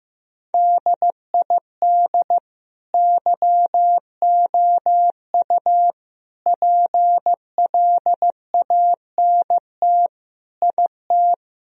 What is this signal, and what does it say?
Morse code, heard as a keyed tone, about 15 wpm, 700 Hz. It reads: DID YOU PLANT IT
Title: Sentences from Top 1000 Words - Review Part 1 - 15wpm